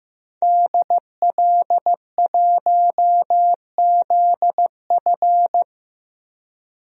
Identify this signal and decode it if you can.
Morse code, heard as a keyed tone, about 15 wpm, 700 Hz. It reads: DL1ZF
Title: DXCC-100 Call Signs - 5 Characters - 15wpm